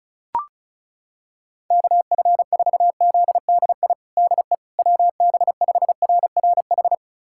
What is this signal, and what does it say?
Morse code, heard as a keyed tone, about 34 wpm, 700 Hz. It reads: KF4ZDI DE WB5RRH
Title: USA Callsign DE Callsign - 6 Characters - 35wpm